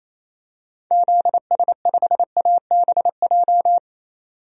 Morse code, heard as a keyed tone, about 28 wpm, 700 Hz. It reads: ZS5ABJ
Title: DXCC-100 Call Signs - 6 Characters - 28wpm